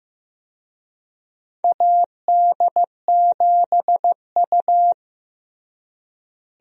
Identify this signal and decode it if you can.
Morse code, heard as a keyed tone, about 15 wpm, 700 Hz. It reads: AD7U
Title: USA Call Signs - 4 Characters - Repeated 3x - 15wpm